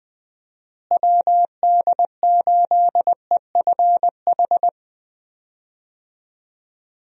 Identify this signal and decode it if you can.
Morse code, heard as a keyed tone, about 19 wpm, 700 Hz. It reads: WD8EFH
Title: USA Call Signs - 6 Characters - Repeated 3x - 20wpm